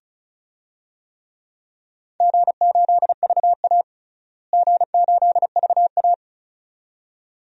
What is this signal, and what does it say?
Morse code, heard as a keyed tone, about 34 wpm, 700 Hz. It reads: G8VA G8VA
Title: DXCC-100 Call Signs - 4 Characters - Repeated 2x - 35wpm